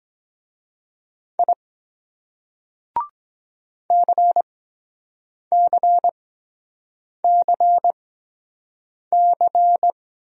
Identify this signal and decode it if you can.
Morse code, heard as a keyed tone, about 23 wpm, spaced as for 7 wpm, 700 Hz. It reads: I CCCC